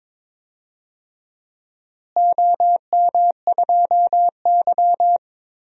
Morse code, heard as a keyed tone, about 22 wpm, 700 Hz. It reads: OM2Y
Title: DXCC-100 Call Signs - 4 Characters - Repeated 3x - 22wpm